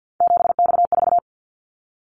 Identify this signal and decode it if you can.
Morse code, heard as a keyed tone, about 48 wpm, 700 Hz. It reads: 7X4
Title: Callsign Prefixes for Top 100 DXCC - 50wpm